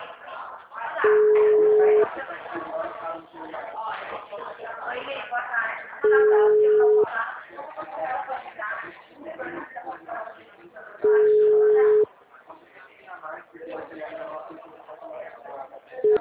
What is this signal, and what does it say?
Ringback tone (Europe, 425 Hz), with speech.